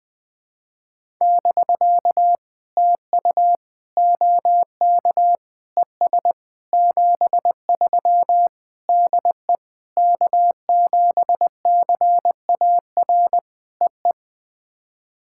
Morse code, heard as a keyed tone, about 20 wpm, 700 Hz. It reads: <BK> TU OK ES 73 DE K7CAR EE
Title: POTA QSO Practice - 20wpm